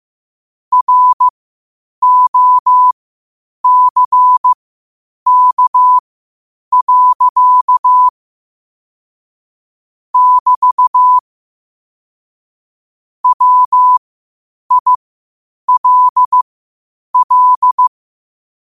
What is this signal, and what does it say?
Morse code, heard as a keyed tone, about 15 wpm, spaced as for 8 wpm, 1000 Hz. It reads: ROCK. <BT> WILL
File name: qotdfiles / 8wpm